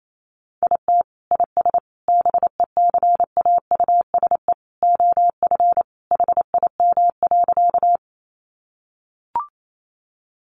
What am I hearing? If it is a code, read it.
Morse code, heard as a keyed tone, about 28 wpm, 700 Hz. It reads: IT IS BECAUSE OF HIM.
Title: Sentences from Top 600 Words 28wpm